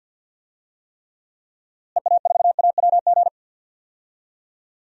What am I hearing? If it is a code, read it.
Morse code, heard as a keyed tone, about 52 wpm, 700 Hz. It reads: EA4AWG
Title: DXCC-100 Call Signs - 6 Characters - Repeated 3x - 50wpm